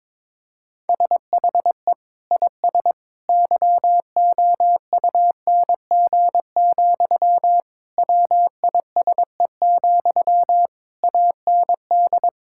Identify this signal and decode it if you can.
Morse code, heard as a keyed tone, about 22 wpm, 700 Hz. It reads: SHE IS YOUNG, WISE, AND
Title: Sentences from Top 1600 Words 22wpm